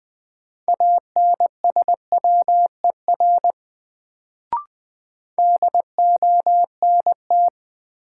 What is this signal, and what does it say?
Morse code, heard as a keyed tone, about 20 wpm, 700 Hz. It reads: ANSWER DONT
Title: Sentences from Top 300 Words - New Word Drill - 20wpm